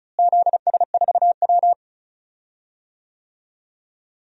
Morse code, heard as a keyed tone, about 34 wpm, 700 Hz. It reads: ZS4W
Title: DXCC-100 Call Signs - 4 Characters - Repeated 2x - 35wpm